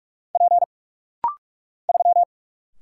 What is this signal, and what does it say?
Morse code, heard as a keyed tone, about 46 wpm, 700 Hz. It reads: P 3
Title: Single Letter-Number - Rapid-Fire 45wpm